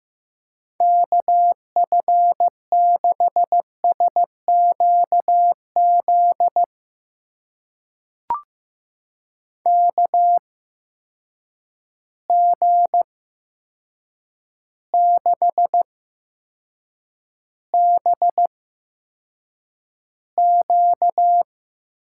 Morse code, heard as a keyed tone, about 15 wpm, spaced as for 7 wpm, 700 Hz. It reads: KF6SQZ KG6BQ
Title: USA Call Signs - 6 Characters - 8x Character Spacing - 15wpm